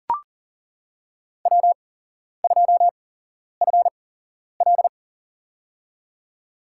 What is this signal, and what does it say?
Morse code, heard as a keyed tone, about 40 wpm, spaced as for 11 wpm, 700 Hz. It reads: W2FL
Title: USA Call Signs - 4 Characters - 8x Character Spacing - 40wpm